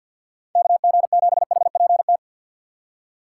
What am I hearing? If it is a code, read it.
Morse code, heard as a keyed tone, about 48 wpm, 690 Hz. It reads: KG7HPT
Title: USA Call Signs - 6 Characters - 3x Character Spacing - 50wpm